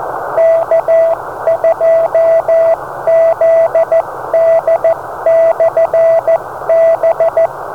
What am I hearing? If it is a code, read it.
Morse code, heard as a keyed tone, about 14 wpm, 660 Hz, with radio noise. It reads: K2ZD/B
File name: beacons / K2ZD